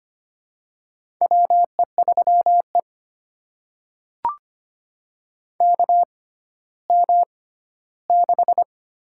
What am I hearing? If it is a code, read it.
Morse code, heard as a keyed tone, about 25 wpm, spaced as for 14 wpm, 700 Hz. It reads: WE3E KM6
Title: USA Call Signs - 6x Character Spacing - 25wpm